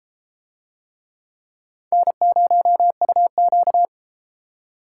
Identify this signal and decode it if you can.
Morse code, heard as a keyed tone, about 31 wpm, 700 Hz. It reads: N0UQ